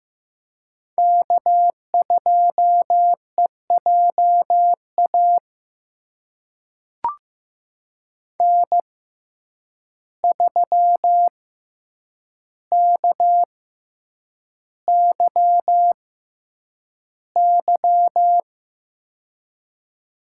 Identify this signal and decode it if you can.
Morse code, heard as a keyed tone, about 15 wpm, spaced as for 8 wpm, 700 Hz. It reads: K2EJA N3KYY
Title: USA Call Signs - 5 Characters - 6x Character Spacing - 15wpm